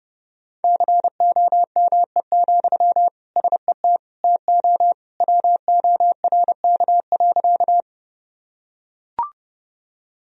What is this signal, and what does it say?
Morse code, heard as a keyed tone, about 30 wpm, 700 Hz. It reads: COME, SET TO WORK.
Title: Sentences from Top 1000 Words - Review Part 1 - 30wpm